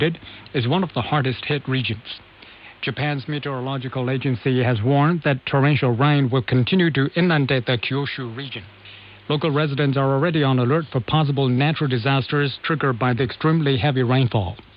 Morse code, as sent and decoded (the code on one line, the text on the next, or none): none